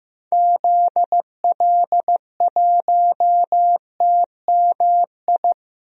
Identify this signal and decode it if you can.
Morse code, heard as a keyed tone, about 15 wpm, 690 Hz. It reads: ZL1TMI
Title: DXCC-100 Call Signs - 6 Characters - 15wpm